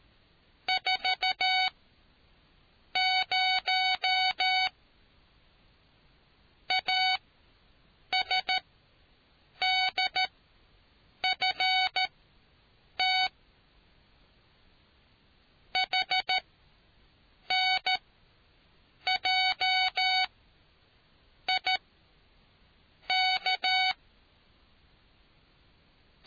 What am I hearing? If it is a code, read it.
Morse code, heard as a keyed tone, about 13 wpm, spaced as for 6 wpm, 750 Hz. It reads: 40 ASDFT HNJIK